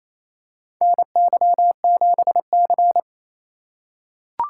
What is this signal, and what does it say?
Morse code, heard as a keyed tone, about 28 wpm, 700 Hz. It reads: NY7C
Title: USA Call Signs - 2x Character Spacing - 28wpm